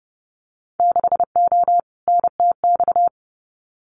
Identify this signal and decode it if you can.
Morse code, heard as a keyed tone, about 30 wpm, 700 Hz. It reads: 6O NTX
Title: Winter Field Day Exchanges Encore 30wpm